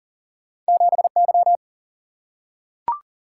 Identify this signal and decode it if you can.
Morse code, heard as a keyed tone, about 40 wpm, 700 Hz. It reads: 7 Y
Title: Single-Letter Single-Number 40wpm